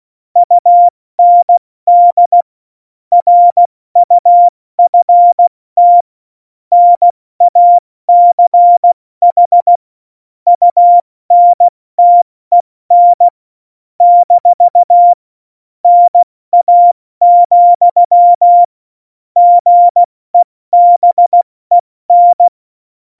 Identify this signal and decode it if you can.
Morse code, heard as a keyed tone, about 16 wpm, 700 Hz. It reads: UND RUFT NACH UNTEN - NA, GEBEN
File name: Morsekurs / Witze / Witz 04 August 2020, 70 F80